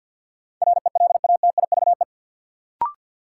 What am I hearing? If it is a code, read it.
Morse code, heard as a keyed tone, about 51 wpm, 700 Hz. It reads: RELATIVE
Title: Sentences from Top 1400 Words - New Word Drill Alt - 50wpm